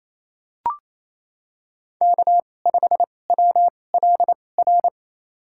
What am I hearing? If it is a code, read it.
Morse code, heard as a keyed tone, about 28 wpm, spaced as for 20 wpm, 700 Hz. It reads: K5WLR
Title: USA Call Signs - 5 Characters - 2x Character Spacing - 28wpm